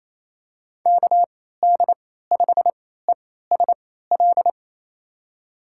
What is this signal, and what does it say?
Morse code, heard as a keyed tone, about 28 wpm, spaced as for 16 wpm, 700 Hz. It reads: KD5ESL
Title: USA Call Signs - 6 Characters - 3x Character Spacing - 28wpm